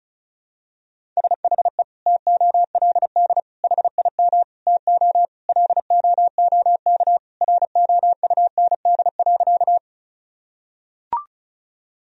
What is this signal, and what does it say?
Morse code, heard as a keyed tone, about 34 wpm, 700 Hz. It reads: SHE TOLD HIM TO LOOK ROUND.
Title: Sentences from Top 1000 Words - Review Part 1 - 35wpm